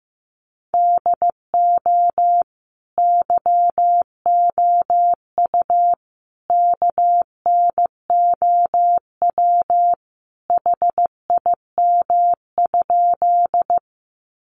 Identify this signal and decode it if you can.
Morse code, heard as a keyed tone, about 15 wpm, 700 Hz. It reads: DO YOU KNOW HIM?
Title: Sentences from Top 100 Words 15wpm